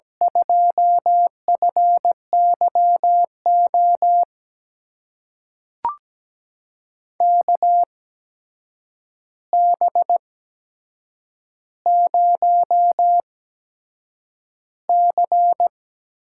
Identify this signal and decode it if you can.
Morse code, heard as a keyed tone, about 17 wpm, spaced as for 7 wpm, 700 Hz. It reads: 2FYO KB0C
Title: USA Call Signs - 6 Characters - 8x Character Spacing - 17wpm